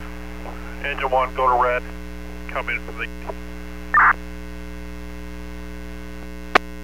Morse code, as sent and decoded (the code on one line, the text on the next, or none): none